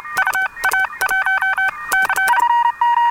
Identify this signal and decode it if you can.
Morse code, heard as a keyed tone, about 27 wpm, spaced as for 23 wpm, 1550 Hz, with noise. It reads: UA1C